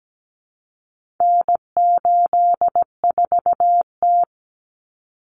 Morse code, heard as a keyed tone, about 17 wpm, 700 Hz. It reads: N84T
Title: Common Cut Numbers - 4-Digit 17wpm